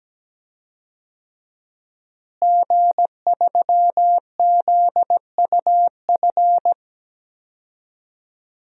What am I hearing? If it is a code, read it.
Morse code, heard as a keyed tone, about 17 wpm, 700 Hz. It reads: G3ZUF
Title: DXCC-100 Call Signs - 5 Characters - Repeated 2x - 17wpm